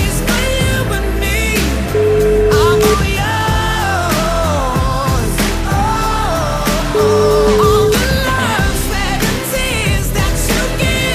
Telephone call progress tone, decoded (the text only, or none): ringback tone